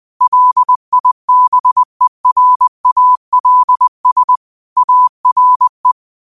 Morse code, heard as a keyed tone, about 20 wpm, 1000 Hz. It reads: LIBERALS ARE
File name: qotdfiles / 20wpm